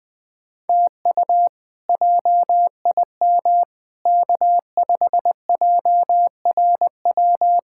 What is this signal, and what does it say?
Morse code, heard as a keyed tone, about 20 wpm, 700 Hz. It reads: TU JIM K5JRW
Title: K1USN SST QSO Practice - 20wpm